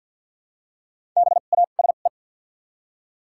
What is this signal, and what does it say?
Morse code, heard as a keyed tone, about 53 wpm, spaced as for 37 wpm, 700 Hz. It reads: BASE
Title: 2x Character Spacing - Top 500 Words - 50wpm